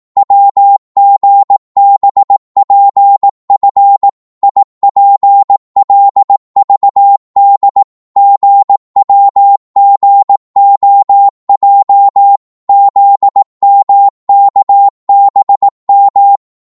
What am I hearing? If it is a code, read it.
Morse code, heard as a keyed tone, about 18 wpm, 800 Hz. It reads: WGBPF IPLVD GWGOJ ZMKBM
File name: Morse / ltr-18wpm